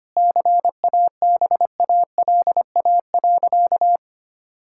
Morse code, heard as a keyed tone, about 25 wpm, 700 Hz. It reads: CABALA.
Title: Pangram Challenge v2 - 25wpm